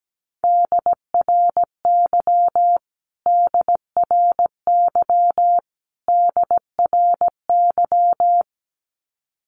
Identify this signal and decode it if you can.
Morse code, heard as a keyed tone, about 17 wpm, 700 Hz. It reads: DRY DRY DRY
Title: Top 500 Words Repeated 17wpm